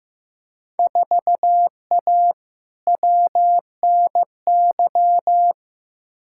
Morse code, heard as a keyed tone, about 15 wpm, 700 Hz. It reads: 4A WNY
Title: Field Day Exchanges Encore - 15wpm